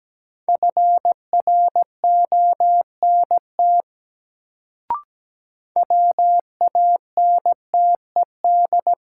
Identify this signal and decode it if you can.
Morse code, heard as a keyed tone, about 17 wpm, 700 Hz. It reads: FRONT WANTED
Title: Sets of 2 Words - New Word Drill - 17wpm